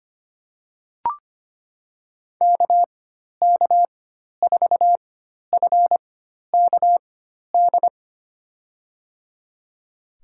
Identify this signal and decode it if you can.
Morse code, heard as a keyed tone, about 25 wpm, spaced as for 12 wpm, 700 Hz. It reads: KK4FKD
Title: USA Call Signs - 4x Character Spacing - 25wpm